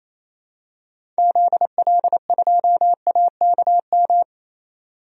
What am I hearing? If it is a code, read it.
Morse code, heard as a keyed tone, about 28 wpm, 700 Hz. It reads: ZL2AKM
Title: DXCC-100 Call Signs - 6 Characters - 28wpm